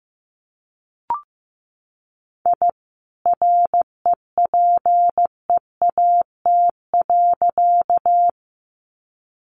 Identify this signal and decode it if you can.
Morse code, heard as a keyed tone, about 15 wpm, 700 Hz. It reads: I REPEAT.
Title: A Sound Of Thunder 15wpm